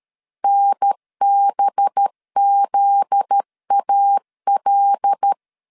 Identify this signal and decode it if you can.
Morse code, heard as a keyed tone, about 13 wpm, 800 Hz. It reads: NBZAL
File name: CW / DET / 1 9 1500C S C 12wpm 3811